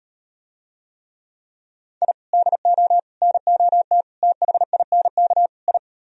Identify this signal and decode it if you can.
Morse code, heard as a keyed tone, about 38 wpm, 700 Hz. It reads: I DO NOT THINK I